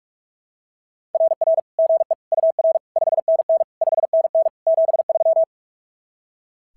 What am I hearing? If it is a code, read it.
Morse code, heard as a keyed tone, about 44 wpm, 660 Hz. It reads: RR GE UR 5NN 5NN 73
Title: SOTA QSO Practice 45wpm